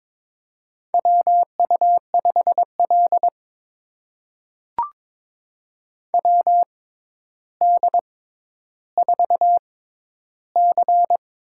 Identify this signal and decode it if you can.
Morse code, heard as a keyed tone, about 22 wpm, spaced as for 11 wpm, 700 Hz. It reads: WU5L WD4C
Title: USA Call Signs - 4 Characters - 6x Character Spacing - 22wpm